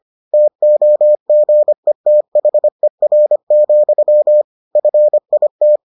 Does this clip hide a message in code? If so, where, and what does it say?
Yes: Morse code from 0.3 to 5.8 s, heard as a keyed tone, about 25 wpm, 600 Hz. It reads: TOGETHER, FIT